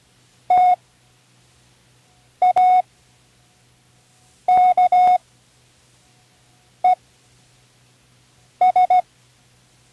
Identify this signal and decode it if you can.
Morse code, heard as a keyed tone, about 16 wpm, spaced as for 5 wpm, 730 Hz. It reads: TAKES
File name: Learning Morse Code (from Tune in the World)